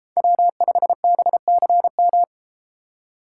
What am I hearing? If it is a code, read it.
Morse code, heard as a keyed tone, about 32 wpm, 700 Hz. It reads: W5BCM